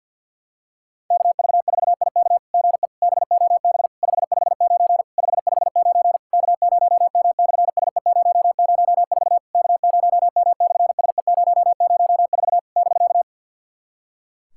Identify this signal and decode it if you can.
Morse code, heard as a keyed tone, about 50 wpm, 700 Hz. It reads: KV4IK GE BOB 559 559 K0M/SE004 K0M/SE004 <BK>
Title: SOTA QSO Practice 50wpm